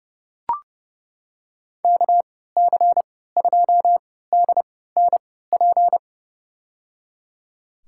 Morse code, heard as a keyed tone, about 30 wpm, spaced as for 17 wpm, 700 Hz. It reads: KC2DNP
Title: USA Call Signs - 6 Characters - 3x Character Spacing - 30wpm